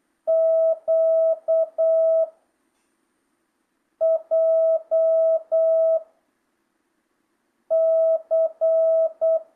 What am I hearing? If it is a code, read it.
Morse code, heard as a keyed tone, about 8 wpm, 660 Hz. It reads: Q J C
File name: jh8cbh / 2025cw